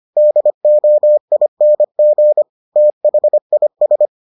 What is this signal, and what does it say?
Morse code, heard as a keyed tone, about 25 wpm, 600 Hz. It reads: DOING THIS